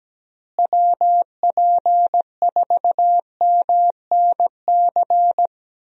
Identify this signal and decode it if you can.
Morse code, heard as a keyed tone, about 17 wpm, 700 Hz. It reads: WP4MNC
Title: USA Call Signs - 6 Characters - Repeated 2x - 17wpm